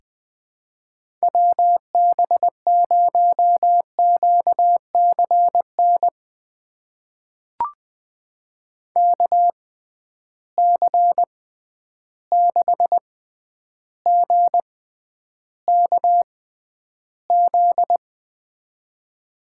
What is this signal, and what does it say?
Morse code, heard as a keyed tone, about 20 wpm, spaced as for 10 wpm, 700 Hz. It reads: WB0QCN KC6GKZ
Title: USA Call Signs - 6 Characters - 6x Character Spacing - 20wpm